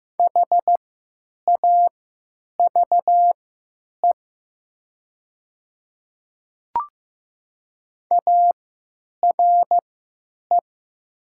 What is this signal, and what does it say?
Morse code, heard as a keyed tone, about 15 wpm, spaced as for 9 wpm, 700 Hz. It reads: HAVE ARE
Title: 3x Character Spacing - Top 200 Words - 15wpm